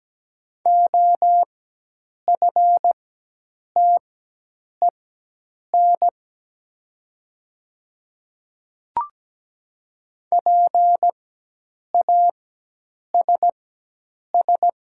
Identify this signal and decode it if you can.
Morse code, heard as a keyed tone, about 17 wpm, spaced as for 8 wpm, 700 Hz. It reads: OFTEN PASS
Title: 4x Character Spacing - Top 500 Words - 17wpm